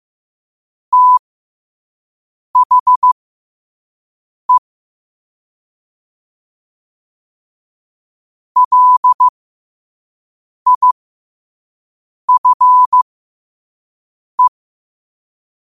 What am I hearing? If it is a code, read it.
Morse code, heard as a keyed tone, about 15 wpm, spaced as for 5 wpm, 1000 Hz. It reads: THE LIFE
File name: qotdfiles / 5wpm